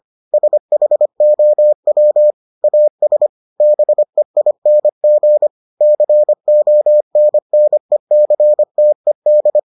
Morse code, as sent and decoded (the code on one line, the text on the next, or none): ... .... --- .-- / .- ... / -... .. -. --. / -.-. --- -. -. -.-. - . -..
SHOW AS BEING CONNECTED